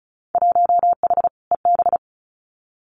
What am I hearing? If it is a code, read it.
Morse code, heard as a keyed tone, about 34 wpm, 700 Hz. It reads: 1H EB
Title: Winter Field Day Exchanges 35wpm